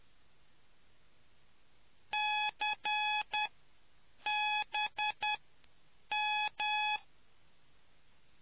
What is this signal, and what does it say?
Morse code, heard as a keyed tone, about 10 wpm, spaced as for 7 wpm, 840 Hz. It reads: CBM